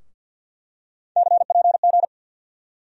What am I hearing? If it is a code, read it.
Morse code, heard as a keyed tone, about 48 wpm, 700 Hz. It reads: CPG
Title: USA Call Sign Postfixes - 3 Letters 50wpm